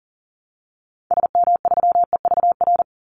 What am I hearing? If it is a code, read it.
Morse code, heard as a keyed tone, about 40 wpm, 700 Hz. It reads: SM3EVR
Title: Call Signs Repeated 2x - 40wpm